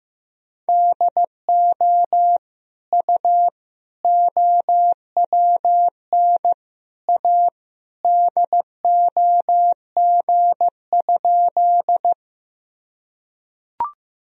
Morse code, heard as a keyed tone, about 16 wpm, 700 Hz. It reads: DO U OWN A DOG?
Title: Sentences from Top 400 Words and Abbreviations 15wpm